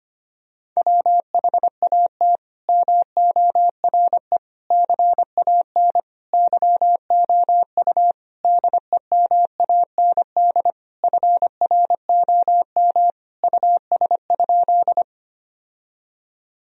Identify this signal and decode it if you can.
Morse code, heard as a keyed tone, about 25 wpm, 700 Hz. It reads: WHAT MORE CAN YOU DEMAND FROM US?
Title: Sentences from Top 1000 Words - Review Part 4 - 25wpm